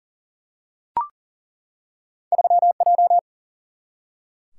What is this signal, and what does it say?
Morse code, heard as a keyed tone, about 40 wpm, 700 Hz. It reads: 3J
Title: Single-Letter Single-Number 40wpm